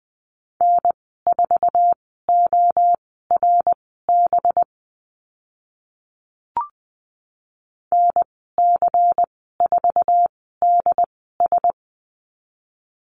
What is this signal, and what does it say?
Morse code, heard as a keyed tone, about 20 wpm, spaced as for 14 wpm, 700 Hz. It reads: N4ORB NC4DS
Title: 186 Lesson 30g - Letter b - Call Signs 20wpm